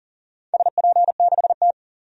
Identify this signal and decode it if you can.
Morse code, heard as a keyed tone, about 40 wpm, 700 Hz. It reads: SP6T
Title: DXCC-100 Call Signs - 4 Characters - 40wpm